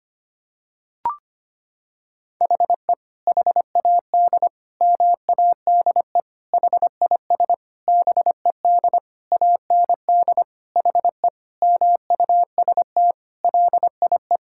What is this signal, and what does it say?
Morse code, heard as a keyed tone, about 25 wpm, 700 Hz. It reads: HE HAD MADE HIS BED AND HE MUST LIE
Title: Sentences from Top 1600 Words 25wpm